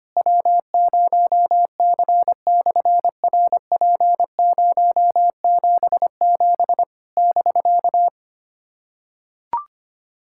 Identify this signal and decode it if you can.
Morse code, heard as a keyed tone, about 25 wpm, 700 Hz. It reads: W0C/RP077 <BK>
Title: SOTA QSO Practice 25wpm